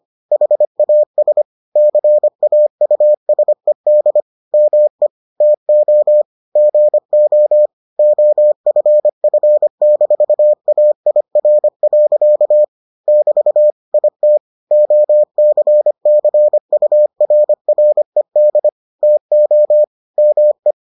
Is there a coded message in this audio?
Yes: Morse code from 0.3 to 20.7 s, heard as a keyed tone, about 25 wpm, 600 Hz. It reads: HAS CAUSED ME TO GO OFF-AIR. <BT> IT OCCURRED TO ME